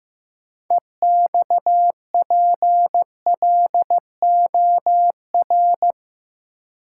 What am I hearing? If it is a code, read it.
Morse code, heard as a keyed tone, about 15 wpm, 700 Hz. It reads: EXPLOR
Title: Sentences from Top 1100 Words - New Word Drill - 15wpm